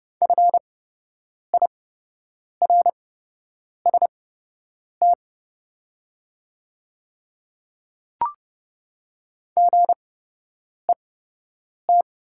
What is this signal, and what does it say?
Morse code, heard as a keyed tone, about 30 wpm, spaced as for 8 wpm, 690 Hz. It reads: FIRST GET
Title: 8x Character Spacing - Top 200 Words - 30wpm